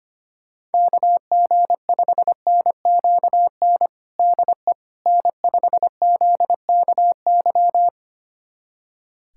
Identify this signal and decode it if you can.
Morse code, heard as a keyed tone, about 25 wpm, 700 Hz. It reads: KG5NQN DE N5ZKY
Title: USA Callsign DE Callsign 25wpm